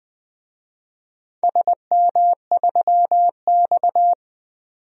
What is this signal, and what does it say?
Morse code, heard as a keyed tone, about 20 wpm, 700 Hz. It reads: SM3X
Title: DXCC-100 Call Signs - 4 Characters - 20wpm